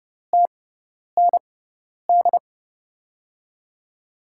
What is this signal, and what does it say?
Morse code, heard as a keyed tone, about 30 wpm, spaced as for 10 wpm, 700 Hz. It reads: TND
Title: USA Call Signs - 6 Characters - 6x Character Spacing - 30wpm